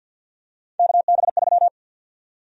Morse code, heard as a keyed tone, about 53 wpm, 700 Hz. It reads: KB3
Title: USA Call Sign Prefixes - 50wpm